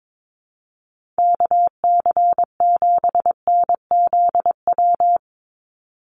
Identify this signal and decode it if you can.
Morse code, heard as a keyed tone, about 22 wpm, 700 Hz. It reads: KC7NZW
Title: USA Call Signs 22wpm